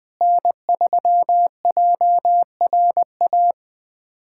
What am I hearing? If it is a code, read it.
Morse code, heard as a keyed tone, about 20 wpm, 700 Hz. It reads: N3JRA
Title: USA Callsign DE Callsign - 5 Characters - 20wpm